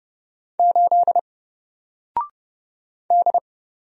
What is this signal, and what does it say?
Morse code, heard as a keyed tone, about 30 wpm, 700 Hz. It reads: 8 D
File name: Single-Letter-Number-30wpm